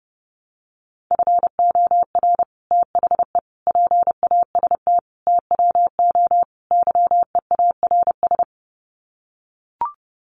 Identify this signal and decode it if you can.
Morse code, heard as a keyed tone, about 28 wpm, 700 Hz. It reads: FOR THE PAST TWO YEARS
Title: Sets of 5 Words Encore2 30wpm